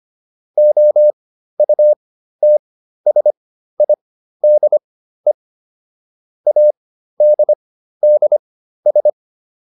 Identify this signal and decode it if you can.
Morse code, heard as a keyed tone, about 25 wpm, spaced as for 13 wpm, 600 Hz. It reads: OUTSIDE ADDS